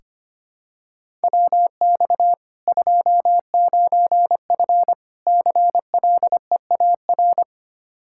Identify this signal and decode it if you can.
Morse code, heard as a keyed tone, about 25 wpm, 700 Hz. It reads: WX 29F CLEAR
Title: Practice QSO - Exchange 2 - Info - 25wpm